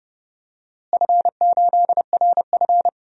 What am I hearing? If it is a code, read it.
Morse code, heard as a keyed tone, about 32 wpm, 700 Hz. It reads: F8RF
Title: DXCC-100 Call Signs - 4 Characters - Repeated 3x - 30wpm